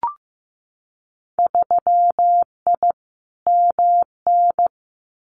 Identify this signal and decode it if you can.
Morse code, heard as a keyed tone, about 15 wpm, 700 Hz. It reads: 3I MN
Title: Winter Field Day Exchanges 15wpm